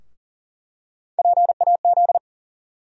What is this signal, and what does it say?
Morse code, heard as a keyed tone, about 40 wpm, 700 Hz. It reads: PAZ
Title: USA Call Sign Postfixes - 3 Letters 40wpm